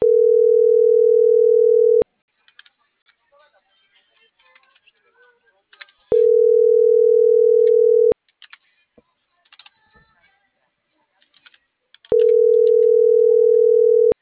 Ringback tone (North America), 440 + 480 Hz.
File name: CENTRAL NMP / out melanie.diaznmp - .45946